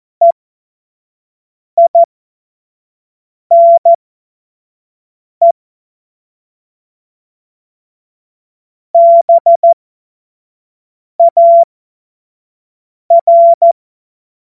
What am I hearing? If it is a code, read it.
Morse code, heard as a keyed tone, about 14 wpm, spaced as for 5 wpm, 680 Hz. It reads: EINE BAR
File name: Morsekurs / Witze / Witz 04. April 2023 25 F70